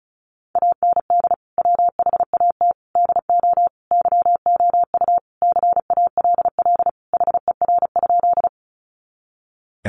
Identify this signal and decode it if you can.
Morse code, heard as a keyed tone, about 34 wpm, 700 Hz. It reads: AND WHAT DO YOU CALL HER?
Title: Sentences from Top 100 Words 35wpm